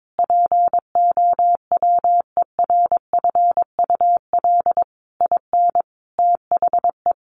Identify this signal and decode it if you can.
Morse code, heard as a keyed tone, about 22 wpm, 700 Hz. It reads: POWERFUL IN THE
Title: Sentences from Top 1300 Words 22wpm